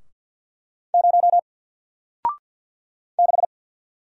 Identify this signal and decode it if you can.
Morse code, heard as a keyed tone, about 48 wpm, 700 Hz. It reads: 0 6
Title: Single Letter-Number Alt 50wpm